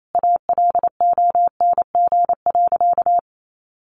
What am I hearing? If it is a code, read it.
Morse code, heard as a keyed tone, about 28 wpm, 700 Hz. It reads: ALONG.
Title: Sentences from Top 1500 Words 28wpm